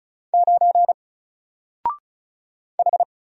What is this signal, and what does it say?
Morse code, heard as a keyed tone, about 33 wpm, 700 Hz. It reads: 9 H
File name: Single-Letter-Number-35wpm